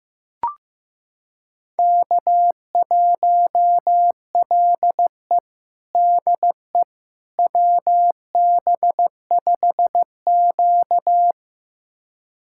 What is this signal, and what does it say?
Morse code, heard as a keyed tone, about 15 wpm, 700 Hz. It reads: K1LE DE WB5Q
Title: USA Callsign DE Callsign - 4 Characters - 15wpm